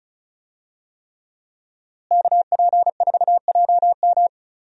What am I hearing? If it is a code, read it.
Morse code, heard as a keyed tone, about 36 wpm, 700 Hz. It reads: KP4JM
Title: DXCC-100 Call Signs - 5 Characters - Repeated 3x - 35wpm